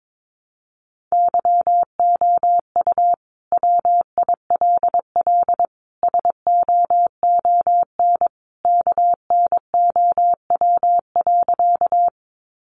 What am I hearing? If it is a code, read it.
Morse code, heard as a keyed tone, about 22 wpm, 700 Hz. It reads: YOU WILL SOON KNOW.
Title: Sentences from Top 400 Words 22wpm